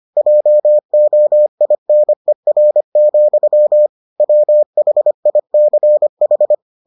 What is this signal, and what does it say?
Morse code, heard as a keyed tone, about 25 wpm, 600 Hz. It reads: JOINER, WHICH